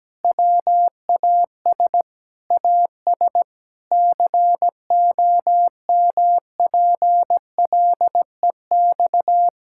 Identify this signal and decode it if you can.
Morse code, heard as a keyed tone, about 17 wpm, 700 Hz. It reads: WAS AS COMPLEX
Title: Sentences from Top 1000 Words - Review Part 6 - 17wpm